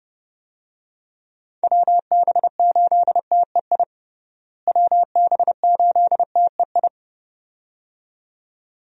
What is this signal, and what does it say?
Morse code, heard as a keyed tone, about 30 wpm, 700 Hz. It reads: WB8TEI WB8TEI
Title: USA Call Signs - 6 Characters - Repeated 2x - 30wpm